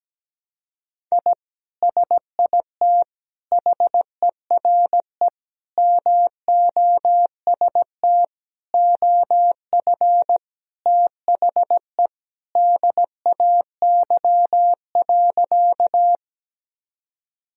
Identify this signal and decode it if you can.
Morse code, heard as a keyed tone, about 17 wpm, 700 Hz. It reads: I SIT HERE MOST OF THE DAY.
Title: Sentences from Top 1000 Words - Review Part 1 - 17wpm